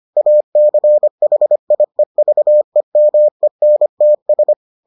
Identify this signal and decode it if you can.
Morse code, heard as a keyed tone, about 25 wpm, 600 Hz. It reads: ACHIEVEMENTS